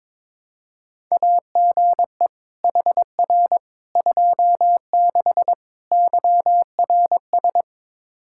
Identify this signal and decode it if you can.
Morse code, heard as a keyed tone, about 22 wpm, 700 Hz. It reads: AGE HR 26 YRS
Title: Practice QSO - Exchange 3 - More Info - 22wpm